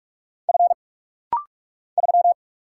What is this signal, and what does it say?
Morse code, heard as a keyed tone, about 44 wpm, 700 Hz. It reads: F 3
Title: Single Letter-Number Alt - Rapid-Fire 45wpm